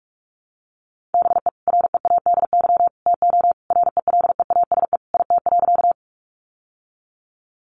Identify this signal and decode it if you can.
Morse code, heard as a keyed tone, about 44 wpm, 700 Hz. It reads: BE READY TO RELEASE IT.
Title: Sentences from Top 1100 Words 45wpm